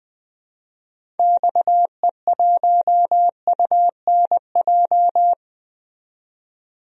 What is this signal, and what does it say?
Morse code, heard as a keyed tone, about 20 wpm, 700 Hz. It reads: XE1UNJ
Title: DXCC-100 Call Signs - 6 Characters - 20wpm